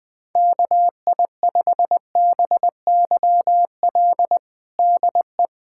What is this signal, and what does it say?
Morse code, heard as a keyed tone, about 21 wpm, 700 Hz. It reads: KI5BYL DE
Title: USA Callsign DE Callsign - 6 Characters - 20wpm